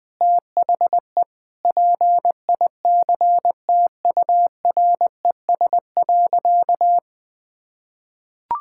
Morse code, heard as a keyed tone, about 20 wpm, 700 Hz. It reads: THE PICTURES.
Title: Sentences with 7-Word Phrases - 20wpm